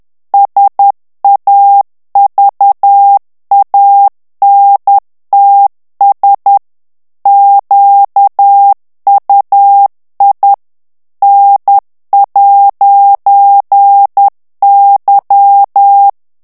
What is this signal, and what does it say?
Morse code, heard as a keyed tone, about 10 wpm, 800 Hz. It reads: SAVANTS QUI N'Y
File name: CW / FICHIERS-MP3 / FRANCAIS TEXTE CLAIR 2 VITESSE 10 MOTS MINUTES